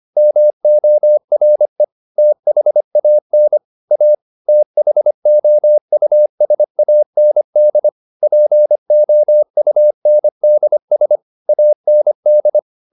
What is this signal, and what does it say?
Morse code, heard as a keyed tone, about 25 wpm, 600 Hz. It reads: MORE THAN A THOUSAND POUNDS AND
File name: the old man and the sea part 1 25wpm